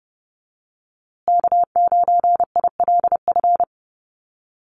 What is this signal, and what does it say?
Morse code, heard as a keyed tone, about 30 wpm, 700 Hz. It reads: K9ILF
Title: USA Call Signs Repeated 3x 30wpm